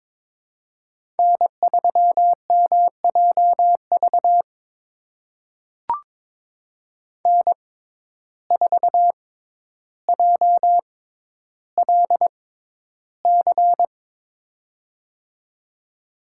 Morse code, heard as a keyed tone, about 22 wpm, spaced as for 11 wpm, 700 Hz. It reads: N3MJV N4JLC
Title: USA Call Signs - 5 Characters - 6x Character Spacing - 22wpm